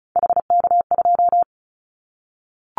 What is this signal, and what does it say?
Morse code, heard as a keyed tone, about 34 wpm, 700 Hz. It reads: HK2
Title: Callsign Prefixes for Top 100 DXCC - 35wpm